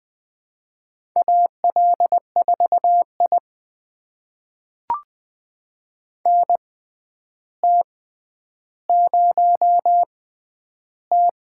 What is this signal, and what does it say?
Morse code, heard as a keyed tone, about 20 wpm, spaced as for 10 wpm, 700 Hz. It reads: AL4I NT0T
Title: USA Call Signs - 6x Character Spacing - 20wpm